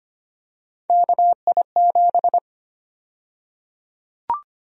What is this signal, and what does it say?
Morse code, heard as a keyed tone, about 25 wpm, 700 Hz. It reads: KI7
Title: USA Call Sign Prefixes - 25wpm